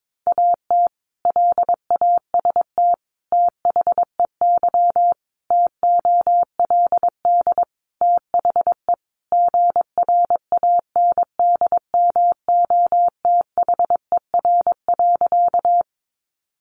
Morse code, heard as a keyed tone, about 22 wpm, 700 Hz. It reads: AT LAST THEY TOLD THE GRANDMOTHER.
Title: Sentences from Top 1200 Words 22wpm